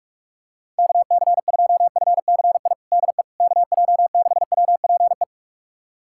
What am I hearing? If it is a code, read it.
Morse code, heard as a keyed tone, about 42 wpm, 700 Hz. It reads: KC2FCI DE KJ6WPE
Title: USA Callsign DE Callsign 45wpm